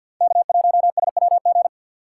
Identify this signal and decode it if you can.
Morse code, heard as a keyed tone, about 49 wpm, 700 Hz. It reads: K1SWG
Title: USA Call Signs - 4x Character Spacing - 50wpm